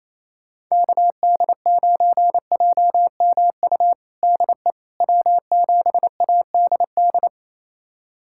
Morse code, heard as a keyed tone, about 28 wpm, 700 Hz. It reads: KD9JMU DE W7ADD
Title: USA Callsign DE Callsign 28wpm